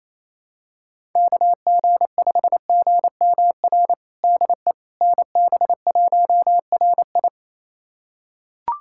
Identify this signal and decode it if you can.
Morse code, heard as a keyed tone, about 28 wpm, 700 Hz. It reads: KG5GMR DE NB1RI
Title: USA Callsign DE Callsign 28wpm